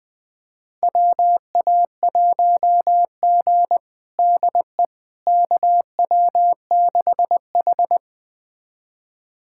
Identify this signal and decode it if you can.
Morse code, heard as a keyed tone, about 20 wpm, 700 Hz. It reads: WA1G DE KW6H
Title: USA Callsign DE Callsign - 4 Characters - 20wpm